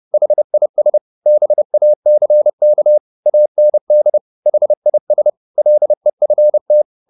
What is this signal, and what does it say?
Morse code, heard as a keyed tone, about 30 wpm, 600 Hz. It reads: HIS BACK AND HIS LEFT